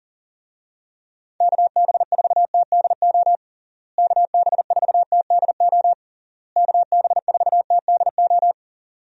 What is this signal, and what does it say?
Morse code, heard as a keyed tone, about 40 wpm, 700 Hz. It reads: KB4TDO KB4TDO KB4TDO
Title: USA Call Signs - 6 Characters - Repeated 3x - 40wpm